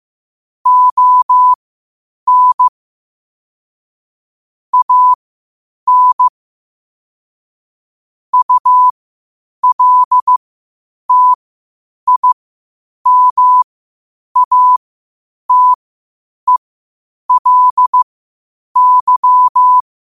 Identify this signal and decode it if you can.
Morse code, heard as a keyed tone, about 15 wpm, spaced as for 8 wpm, 1000 Hz. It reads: ON AN ULTIMATELY
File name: qotdfiles / 8wpm